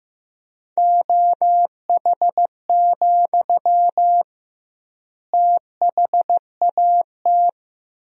Morse code, heard as a keyed tone, about 15 wpm, 700 Hz. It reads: OH, THAT
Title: 2x Word-Spacing - Sentences from Top 500 Words - 15wpm